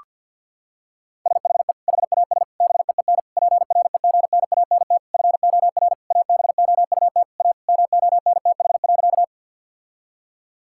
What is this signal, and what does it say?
Morse code, heard as a keyed tone, about 50 wpm, 700 Hz. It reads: SHE HAS BEEN PREGNANT FOR ABOUT A MONTH.
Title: Sentences from Top 1000 Words - Review Part 6 - 50wpm